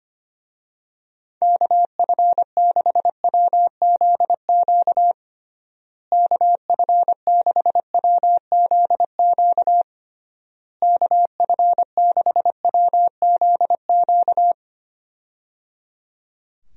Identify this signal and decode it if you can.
Morse code, heard as a keyed tone, about 25 wpm, 700 Hz. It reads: KF6WZQ KF6WZQ KF6WZQ
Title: USA Call Signs - 6 Characters - Repeated 3x - 25wpm